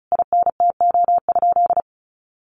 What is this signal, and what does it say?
Morse code, heard as a keyed tone, about 34 wpm, 700 Hz. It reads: INTO?
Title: Sentences from Top 600 Words 35wpm